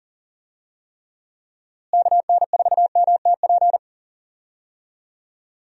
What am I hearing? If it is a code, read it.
Morse code, heard as a keyed tone, about 37 wpm, 700 Hz. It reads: KN4MTP
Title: USA Call Signs - 6 Characters - Repeated 3x - 40wpm